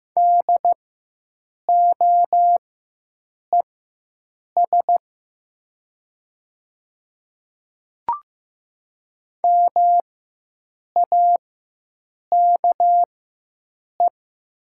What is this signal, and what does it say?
Morse code, heard as a keyed tone, about 15 wpm, spaced as for 7 wpm, 700 Hz. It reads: DOES MAKE
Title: 4x Character Spacing - Top 200 Words - 15wpm